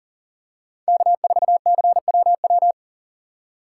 Morse code, heard as a keyed tone, about 40 wpm, 700 Hz. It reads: K4CWW
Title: USA Call Signs - 6x Character Spacing - 40wpm